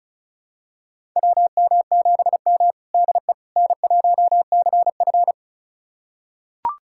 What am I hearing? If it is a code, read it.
Morse code, heard as a keyed tone, about 36 wpm, 700 Hz. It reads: WM7M DE N1CF
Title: USA Callsign DE Callsign - 4 Characters - 35wpm